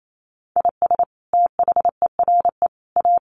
Morse code, heard as a keyed tone, about 28 wpm, 700 Hz. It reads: IS THERE A
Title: Sentences from Top 1200 Words 28wpm